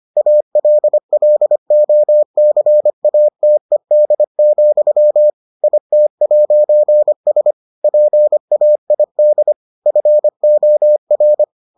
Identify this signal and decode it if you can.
Morse code, heard as a keyed tone, about 25 wpm, 600 Hz. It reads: ALLOCATED, IT'S PAID FOR